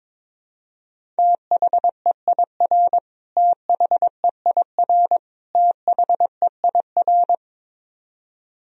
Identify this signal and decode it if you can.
Morse code, heard as a keyed tone, about 22 wpm, 700 Hz. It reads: THEIR THEIR THEIR
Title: Top 100 Words Repeated 22wpm